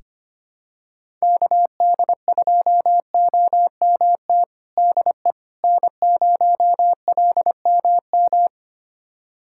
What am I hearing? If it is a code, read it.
Morse code, heard as a keyed tone, about 25 wpm, 700 Hz. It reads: KD2OMT DE N0LMM
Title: USA Callsign DE Callsign 25wpm